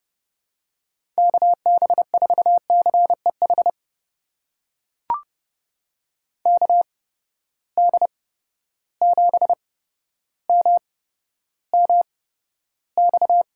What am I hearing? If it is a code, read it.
Morse code, heard as a keyed tone, about 30 wpm, spaced as for 13 wpm, 700 Hz. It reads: KB4CEH KD7MMX
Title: USA Call Signs - 6 Characters - 8x Character Spacing - 30wpm